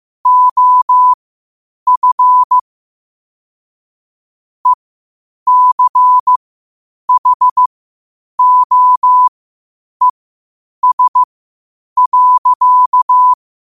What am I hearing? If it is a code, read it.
Morse code, heard as a keyed tone, about 15 wpm, spaced as for 8 wpm, 1000 Hz. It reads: OF ECHOES.
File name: qotdfiles / 8wpm